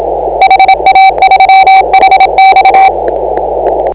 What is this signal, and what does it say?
Morse code, heard as a keyed tone, about 27 wpm, 770 Hz. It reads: HA3HX
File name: AUDIO2019 / HA8KAZ CW